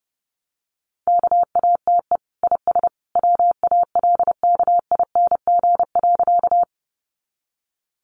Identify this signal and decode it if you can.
Morse code, heard as a keyed tone, about 30 wpm, 700 Hz. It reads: KATE IS WALKING.
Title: Easy Phrases 30wpm